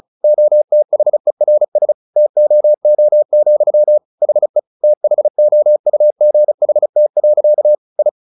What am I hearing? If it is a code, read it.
Morse code, heard as a keyed tone, about 34 wpm, 600 Hz. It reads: OTHERS TOO, HE THOUGHT. I
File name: the old man and the sea 2 35wpm